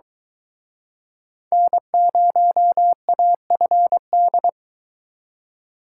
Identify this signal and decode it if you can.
Morse code, heard as a keyed tone, about 24 wpm, 700 Hz. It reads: N0AFD